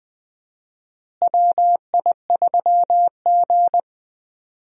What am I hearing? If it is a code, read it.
Morse code, heard as a keyed tone, about 20 wpm, 700 Hz. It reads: WI3G
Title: USA Call Signs - 4 Characters - Repeated 3x - 20wpm